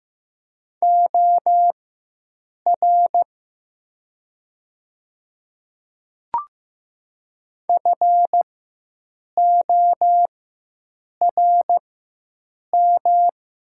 Morse code, heard as a keyed tone, about 15 wpm, spaced as for 7 wpm, 700 Hz. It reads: OR FORM
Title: 4x Character Spacing - Top 200 Words - 15wpm